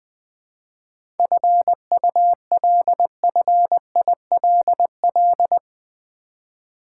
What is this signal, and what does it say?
Morse code, heard as a keyed tone, about 21 wpm, 700 Hz. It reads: FULFILL
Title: 7 Letter Words - First 3 Letters Given - 20wpm